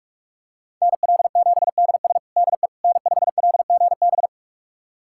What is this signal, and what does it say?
Morse code, heard as a keyed tone, about 46 wpm, 700 Hz. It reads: NL7DS DE N5LGB
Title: USA Callsign DE Callsign - 5 Characters - 45wpm